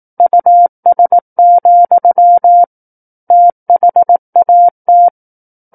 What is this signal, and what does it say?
Morse code, heard as a keyed tone, about 18 wpm, 700 Hz. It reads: US, THAT
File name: 18 WPM